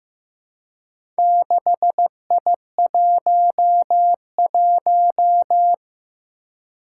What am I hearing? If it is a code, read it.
Morse code, heard as a keyed tone, about 15 wpm, 700 Hz. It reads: 6I11
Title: Four Challenging Letter-Number Combinations 15wpm